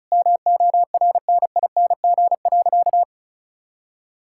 Morse code, heard as a keyed tone, about 34 wpm, 700 Hz. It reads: MORNING.
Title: Sentences from Top 1000 Words - Review Part 6 - 35wpm